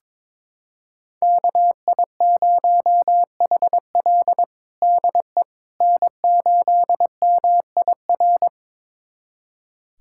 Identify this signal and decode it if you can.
Morse code, heard as a keyed tone, about 21 wpm, 700 Hz. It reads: KI0HL DE N8MIR
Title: USA Callsign DE Callsign - 5 Characters - 22wpm